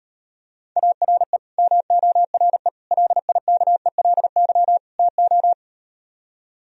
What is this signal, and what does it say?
Morse code, heard as a keyed tone, about 38 wpm, 700 Hz. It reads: ARE MORE LIKELY TO